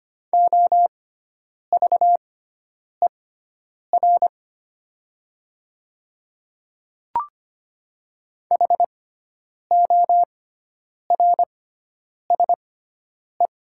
Morse code, heard as a keyed tone, about 25 wpm, spaced as for 9 wpm, 700 Hz. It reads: OVER HORSE
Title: 6x Character Spacing - Top 300 Words - 25wpm